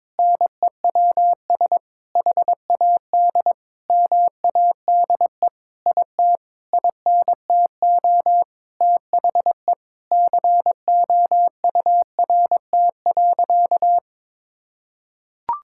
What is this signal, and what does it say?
Morse code, heard as a keyed tone, about 21 wpm, 700 Hz. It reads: NEWS HAD MADE IT INTO THE COURT.
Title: Sentences from Top 1600 Words 22wpm